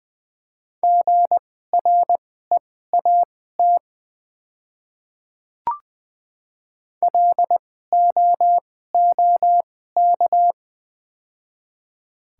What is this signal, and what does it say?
Morse code, heard as a keyed tone, about 20 wpm, spaced as for 14 wpm, 700 Hz. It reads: GREAT LOOK
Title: 2x Character Spacing - Top 500 Words - 20wpm